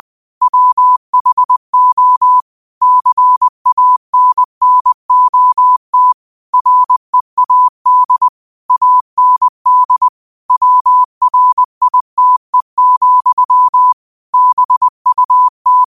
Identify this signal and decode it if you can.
Morse code, heard as a keyed tone, about 20 wpm, 1000 Hz. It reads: WHO CANNOT READ AND WRITE, BUT
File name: qotdfiles / 20wpm